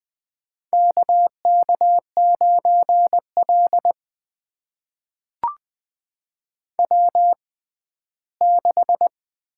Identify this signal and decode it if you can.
Morse code, heard as a keyed tone, about 21 wpm, spaced as for 14 wpm, 700 Hz. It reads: KK9L W6
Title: USA Call Signs - 4 Characters - 6x Character Spacing - 20wpm